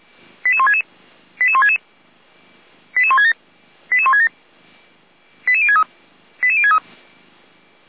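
Five-tone selective call (ZVEI), heard as quick runs of five tones, tones of about 70 ms. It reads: 88166 88166 88167 88167 88063 88063
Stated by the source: ZVEI five-tone selective call received by SDR.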